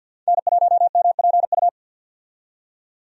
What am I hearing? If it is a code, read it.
Morse code, heard as a keyed tone, about 53 wpm, 700 Hz. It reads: N1MPU
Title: USA Call Signs - 5 Characters - 50wpm